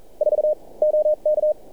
Morse code, heard as a keyed tone, about 32 wpm, 600 Hz. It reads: 4 OK
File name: archived / CRSnet